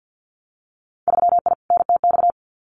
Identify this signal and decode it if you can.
Morse code, heard as a keyed tone, about 50 wpm, 700 Hz. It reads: 3I NTX
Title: Winter Field Day Exchanges 50wpm